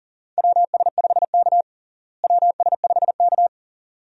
Morse code, heard as a keyed tone, about 40 wpm, 700 Hz. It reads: WS5K WS5K
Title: USA Call Signs - 4 Characters - Repeated 2x - 40wpm